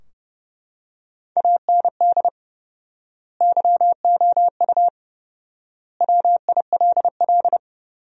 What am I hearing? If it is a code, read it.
Morse code, heard as a keyed tone, about 30 wpm, 700 Hz. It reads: AND YOU WILL
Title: 4x Word-Spacing - Sentences from Top 400 Words - 30wpm